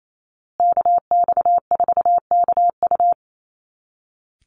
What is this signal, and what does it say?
Morse code, heard as a keyed tone, about 28 wpm, 700 Hz. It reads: KX4KU
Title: Call Signs Repeated 2x - 28wpm